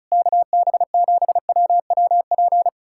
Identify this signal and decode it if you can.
Morse code, heard as a keyed tone, about 40 wpm, 700 Hz. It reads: KB7WWP